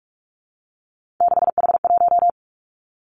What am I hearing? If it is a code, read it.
Morse code, heard as a keyed tone, about 46 wpm, 700 Hz. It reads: 6H1
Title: Three Challenging Letter-Number Combinations 45wpm